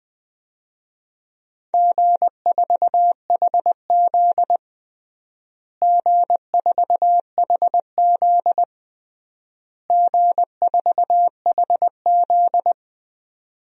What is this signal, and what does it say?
Morse code, heard as a keyed tone, about 21 wpm, 700 Hz. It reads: G4HZ G4HZ G4HZ
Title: DXCC-100 Call Signs - 4 Characters - Repeated 3x - 20wpm